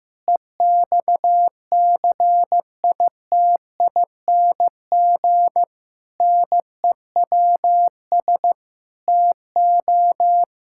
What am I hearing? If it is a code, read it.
Morse code, heard as a keyed tone, about 15 wpm, 700 Hz. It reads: EXCITING NEWS TO
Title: Pangram Challenge v2 - 15wpm